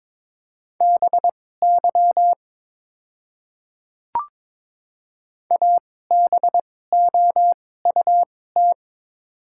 Morse code, heard as a keyed tone, about 22 wpm, spaced as for 16 wpm, 700 Hz. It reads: BY ABOUT
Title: 2x Character Spacing - Top 100 Words - 22wpm